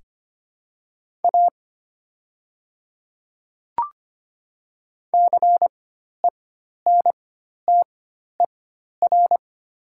Morse code, heard as a keyed tone, about 25 wpm, spaced as for 12 wpm, 700 Hz. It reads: A CENTER
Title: 4x Character Spacing - Top 500 Words - 25wpm